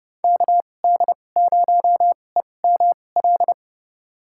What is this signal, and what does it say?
Morse code, heard as a keyed tone, about 30 wpm, spaced as for 22 wpm, 700 Hz. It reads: KD0EML
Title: USA Call Signs - 6 Characters - 2x Character Spacing - 30wpm